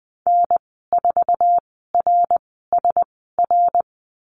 Morse code, heard as a keyed tone, about 20 wpm, spaced as for 15 wpm, 700 Hz. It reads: N4RSR
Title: 062 Lesson 11g - Letter h - Call Signs 20wpm